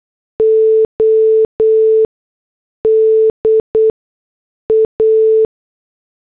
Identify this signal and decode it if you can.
Morse code, heard as a keyed tone, about 8 wpm, spaced as for 6 wpm, 440 Hz. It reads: ODA